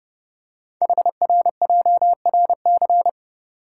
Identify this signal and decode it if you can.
Morse code, heard as a keyed tone, about 30 wpm, 700 Hz. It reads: HRJRC
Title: Code Groups - 5 Letters - 30wpm